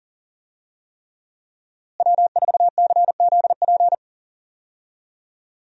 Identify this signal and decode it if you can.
Morse code, heard as a keyed tone, about 39 wpm, 700 Hz. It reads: W4CZP
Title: USA Call Signs - 5 Characters - Repeated 3x - 40wpm